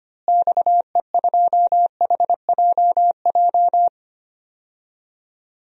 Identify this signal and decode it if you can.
Morse code, heard as a keyed tone, about 25 wpm, 700 Hz. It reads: XE2HJJ
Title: DXCC-100 Call Signs - 6 Characters - Repeated 3x - 25wpm